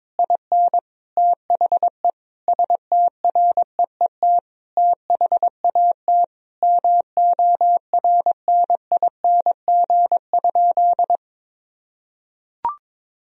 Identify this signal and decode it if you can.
Morse code, heard as a keyed tone, about 22 wpm, 700 Hz. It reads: IN THE STREET THAT MORNING?
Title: Sentences from Top 1000 Words - Review Part 3 - 22wpm